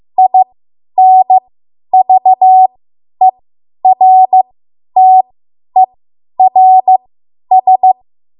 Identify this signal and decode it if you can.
Morse code, heard as a keyed tone, about 15 wpm, spaced as for 10 wpm, 750 Hz. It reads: INVERTERS
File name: Archive / 10 WPM / 10WPM